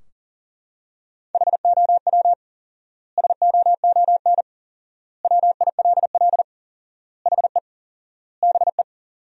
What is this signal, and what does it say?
Morse code, heard as a keyed tone, about 37 wpm, 700 Hz. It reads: HOW SOON WILL HE BE
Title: 4x Word-Spacing - Sentences from Top 400 Words - 40wpm